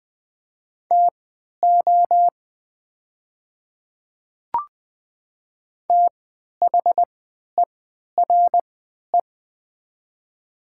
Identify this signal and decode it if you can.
Morse code, heard as a keyed tone, about 20 wpm, spaced as for 11 wpm, 700 Hz. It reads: TO THERE
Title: 3x Character Spacing - Top 100 Words - 20wpm